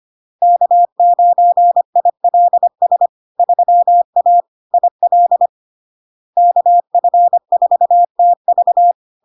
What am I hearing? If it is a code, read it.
Morse code, heard as a keyed tone, about 25 wpm, 700 Hz. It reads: K9ILS 3A IL KF4TV
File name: LARK-FD-2022-Log-in-CW-25wpm-700-Hz